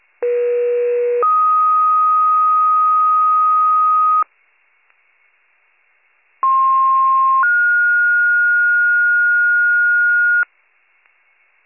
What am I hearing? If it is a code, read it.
Two-tone page, heard as a one-second tone, then a three-second tone, about 480 then 1250 Hz, about 1060 then 1430 Hz. It is heard from 0.2 to 10.4 s.